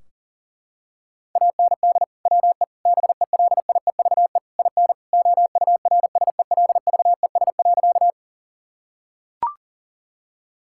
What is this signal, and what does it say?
Morse code, heard as a keyed tone, about 37 wpm, 700 Hz. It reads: AND WE BELIEVE IN OURSELVES.
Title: Sentences from Top 1600 Words 40wpm